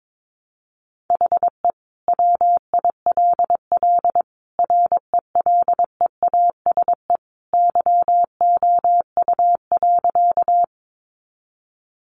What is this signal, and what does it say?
Morse code, heard as a keyed tone, about 22 wpm, 700 Hz. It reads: HE WILL RELEASE YOU.
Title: Sentences from Top 1100 Words 22wpm